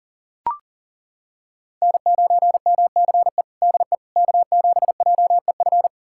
Morse code, heard as a keyed tone, about 40 wpm, 700 Hz. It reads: N9MCE DE K7JEF
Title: USA Callsign DE Callsign - 5 Characters - 40wpm